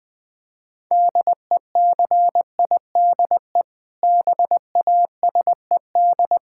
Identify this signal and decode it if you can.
Morse code, heard as a keyed tone, about 21 wpm, 700 Hz. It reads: DECIDE BASED
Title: Sentences with 2-Word Phrases - 20wpm